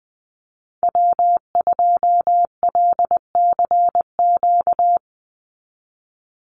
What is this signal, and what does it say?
Morse code, heard as a keyed tone, about 20 wpm, 700 Hz. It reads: W2LCQ
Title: Call Signs - 20wpm